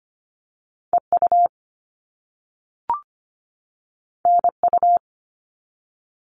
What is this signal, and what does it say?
Morse code, heard as a keyed tone, about 25 wpm, 700 Hz. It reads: EU NU
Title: All Cut Numbers - 2-Digit 25wpm